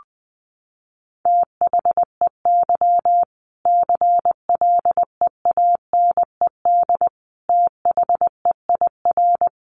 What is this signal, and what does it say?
Morse code, heard as a keyed tone, about 20 wpm, 700 Hz. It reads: THEY CLEANED THEIR
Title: A Sound Of Thunder 20wpm